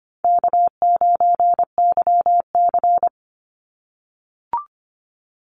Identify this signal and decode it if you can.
Morse code, heard as a keyed tone, about 25 wpm, 700 Hz. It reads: K9YC
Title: Easy Call Signs - 25wpm